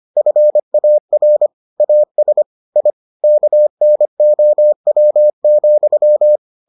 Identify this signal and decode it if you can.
Morse code, heard as a keyed tone, about 25 wpm, 600 Hz. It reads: FAR AS I KNOW,